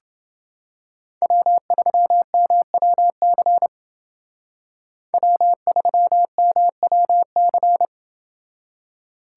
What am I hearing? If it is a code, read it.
Morse code, heard as a keyed tone, about 29 wpm, 700 Hz. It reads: W3MWC W3MWC